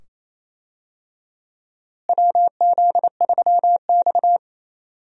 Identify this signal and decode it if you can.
Morse code, heard as a keyed tone, about 27 wpm, 700 Hz. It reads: WZ3X
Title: USA Call Signs - 4 Characters - Repeated 2x - 28wpm